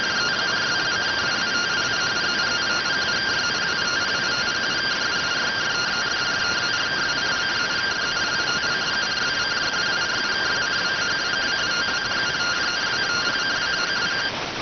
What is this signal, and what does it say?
Two-tone teleprinter signal, about 170 Hz shift.